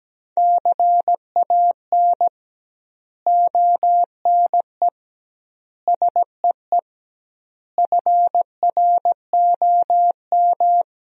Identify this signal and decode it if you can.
Morse code, heard as a keyed tone, about 17 wpm, 700 Hz. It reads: CAN ONE SEE FROM
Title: 2x Word-Spacing - Sentences from Top 100 Words - 17wpm